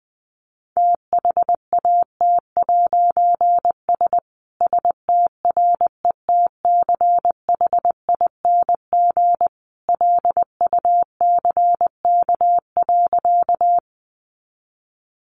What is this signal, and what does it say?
Morse code, heard as a keyed tone, about 20 wpm, 700 Hz. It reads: THAT'S STRETCHING LUCK.
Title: A Sound Of Thunder 20wpm